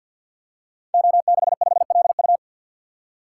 Morse code, heard as a keyed tone, about 51 wpm, 700 Hz. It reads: O65LU
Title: Code Groups - 5 Letter:Numbers - 50wpm